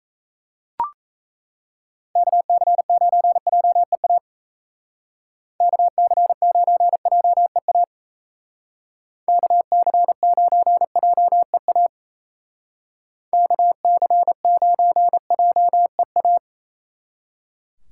Morse code, heard as a keyed tone, about 33 wpm, 700 Hz. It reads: KC9JEA KC9JEA KC9JEA KC9JEA
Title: Speed Racing - USA 2x3 Call Signs 28wpm